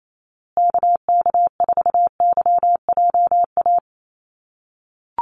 Morse code, heard as a keyed tone, about 28 wpm, 700 Hz. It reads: KK4YJA
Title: USA Call Signs Repeated 2x 28wpm